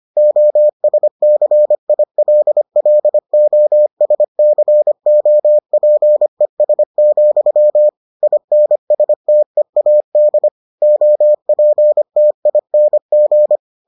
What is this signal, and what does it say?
Morse code, heard as a keyed tone, about 25 wpm, 600 Hz. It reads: OSCILLOSCOPES, INSTEAD OPTING